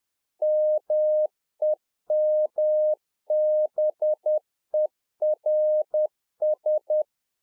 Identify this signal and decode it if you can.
Morse code, heard as a keyed tone, about 10 wpm, 610 Hz. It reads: MEMBERS
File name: 2011-04-mp3 / NEW MEMBERS 10WPM